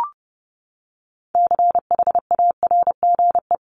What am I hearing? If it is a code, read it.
Morse code, heard as a keyed tone, about 30 wpm, 700 Hz. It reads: CHARGE
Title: Top 751-1000 Words 30wpm